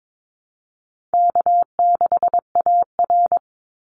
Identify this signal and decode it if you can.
Morse code, heard as a keyed tone, about 22 wpm, 700 Hz. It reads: K6AR
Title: Easy Call Signs Repeated 3x - 22wpm